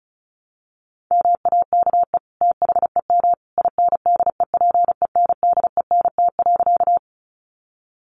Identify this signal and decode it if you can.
Morse code, heard as a keyed tone, about 34 wpm, 700 Hz. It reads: MAKE THEM INDEPENDENT.
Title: Sentences from Top 900 Words 35wpm